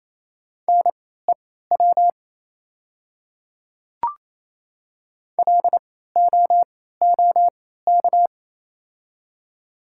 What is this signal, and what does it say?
Morse code, heard as a keyed tone, about 28 wpm, spaced as for 16 wpm, 700 Hz. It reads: NEW LOOK
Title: 3x Character Spacing - Top 300 Words - 28wpm